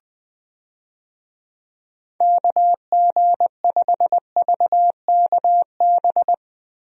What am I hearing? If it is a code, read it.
Morse code, heard as a keyed tone, about 21 wpm, 700 Hz. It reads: KG5VKB
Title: USA Call Signs - 6 Characters - Repeated 3x - 20wpm